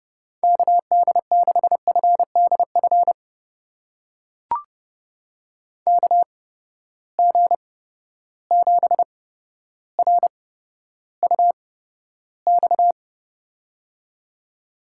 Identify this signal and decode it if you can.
Morse code, heard as a keyed tone, about 32 wpm, spaced as for 13 wpm, 700 Hz. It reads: KD6FDF KG7RUX
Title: USA Call Signs - 6 Characters - 8x Character Spacing - 30wpm